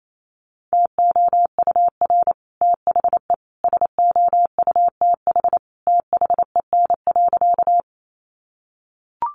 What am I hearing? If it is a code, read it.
Morse code, heard as a keyed tone, about 28 wpm, 700 Hz. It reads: TOUR THE SOUTH THEN.
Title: Sentences from Top 1400 Words 28wpm